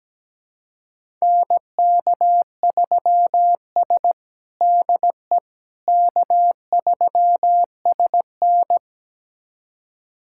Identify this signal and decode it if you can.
Morse code, heard as a keyed tone, about 17 wpm, 700 Hz. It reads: NK3S DE K3SN
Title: USA Callsign DE Callsign - 4 Characters - 17wpm